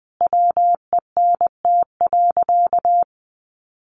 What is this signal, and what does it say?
Morse code, heard as a keyed tone, about 20 wpm, 700 Hz. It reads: WENT.
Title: Sentences from Top 600 Words 20wpm